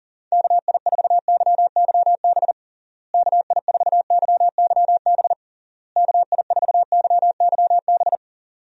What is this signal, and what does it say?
Morse code, heard as a keyed tone, about 37 wpm, 700 Hz. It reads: KI4YYB KI4YYB KI4YYB
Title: USA Call Signs - 6 Characters - Repeated 3x - 40wpm